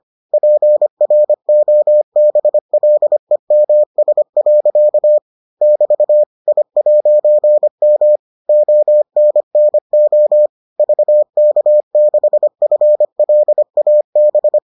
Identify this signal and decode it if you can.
Morse code, heard as a keyed tone, about 25 wpm, 600 Hz. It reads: PROBLEMS. <BT> I'M ONNO VK6FLAB